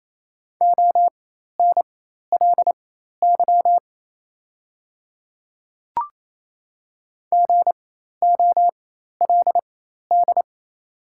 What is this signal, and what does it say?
Morse code, heard as a keyed tone, about 28 wpm, spaced as for 13 wpm, 690 Hz. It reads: ONLY GOLD
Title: 4x Character Spacing - Top 500 Words - 28wpm